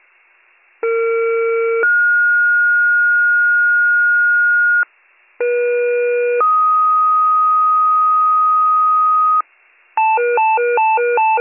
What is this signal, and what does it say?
Two-tone page, heard as a one-second tone, then a three-second tone, about 460 then 1430 Hz, about 480 then 1250 Hz, with noise. It is heard from 0.8 to 9.4 s.